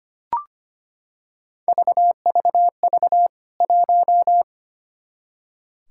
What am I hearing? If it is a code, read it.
Morse code, heard as a keyed tone, about 25 wpm, 700 Hz. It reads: VVV 1
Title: K1USN SST QSO Practice 25wpm